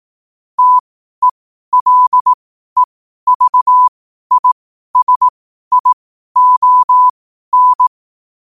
Morse code, heard as a keyed tone, about 18 wpm, spaced as for 12 wpm, 1000 Hz. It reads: TELEVISION
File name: qotdfiles / 12wpm